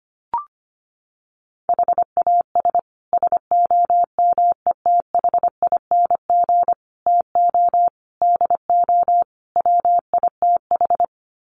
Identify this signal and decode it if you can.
Morse code, heard as a keyed tone, about 25 wpm, 700 Hz. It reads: HAS SOMETHING TO DO WITH
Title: Sets of 5 Words Encore 25wpm